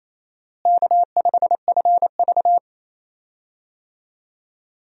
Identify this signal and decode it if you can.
Morse code, heard as a keyed tone, about 28 wpm, 700 Hz. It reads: K5FV
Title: USA Call Signs - 4 Characters - Repeated 3x - 28wpm